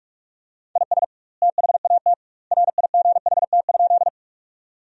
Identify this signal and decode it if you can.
Morse code, heard as a keyed tone, about 46 wpm, 700 Hz. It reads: IS THAT RIGHT?
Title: 2x Word-Spacing - Sentences from Top 200 Words - 45wpm